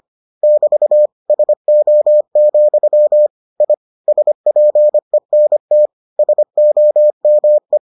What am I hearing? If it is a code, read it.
Morse code, heard as a keyed tone, about 25 wpm, 600 Hz. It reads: <BT> SO, I SPENT SOME